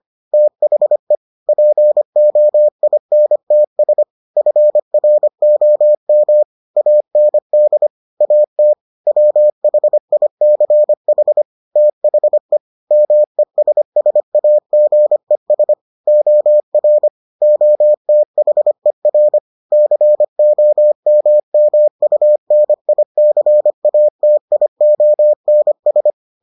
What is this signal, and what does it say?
Morse code, heard as a keyed tone, about 25 wpm, 600 Hz. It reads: THE POINTS FROM AND AT WHICH THE MESSAGES OR OTHER COMMUNICATIONS